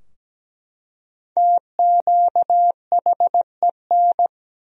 Morse code, heard as a keyed tone, about 17 wpm, 700 Hz. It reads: TQHEN
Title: 2x Word-Spacing - Sentences from Top 100 Words - 17wpm